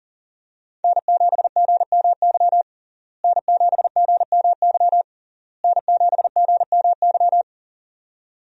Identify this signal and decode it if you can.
Morse code, heard as a keyed tone, about 41 wpm, 700 Hz. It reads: N7GMY N7GMY N7GMY
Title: USA Call Signs - 5 Characters - Repeated 3x - 40wpm